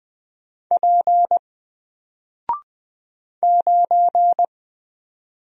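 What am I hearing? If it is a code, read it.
Morse code, heard as a keyed tone, about 20 wpm, 700 Hz. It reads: P 9
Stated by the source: Single letter and number Morse drill at 20 wpm, from N4LCD.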